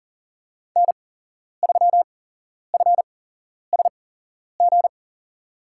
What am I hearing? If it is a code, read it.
Morse code, heard as a keyed tone, about 38 wpm, spaced as for 11 wpm, 700 Hz. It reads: N3FSG
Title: USA Call Signs - 5 Characters - 8x Character Spacing - 40wpm